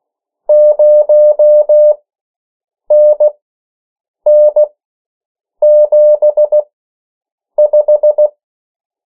Morse code, heard as a keyed tone, about 16 wpm, spaced as for 7 wpm, 600 Hz. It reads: 0NN75
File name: morse mp3 / 14 urok 13